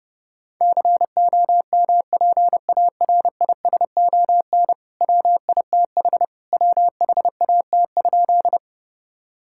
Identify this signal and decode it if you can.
Morse code, heard as a keyed tone, about 30 wpm, 700 Hz. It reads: COMPARISON WITH WHAT?
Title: Sentences from Top 1600 Words 30wpm